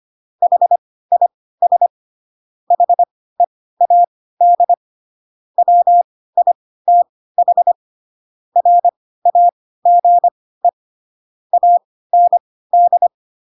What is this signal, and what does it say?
Morse code, heard as a keyed tone, about 25 wpm, spaced as for 16 wpm, 700 Hz. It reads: HIS HEAD WITH RAGE AND
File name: Pinocchio / Pinocchio ch2 10 25 ik1hge